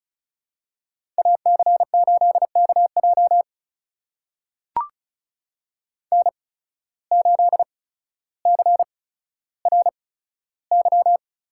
Morse code, heard as a keyed tone, about 36 wpm, spaced as for 15 wpm, 700 Hz. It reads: AC8KJ N8CRY
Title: USA Call Signs - 8x Character Spacing - 35wpm